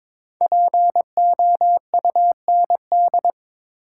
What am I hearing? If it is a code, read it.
Morse code, heard as a keyed tone, about 22 wpm, 700 Hz. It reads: POUND
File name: 5 Letter Words - First 3 Letters Given - 22wpm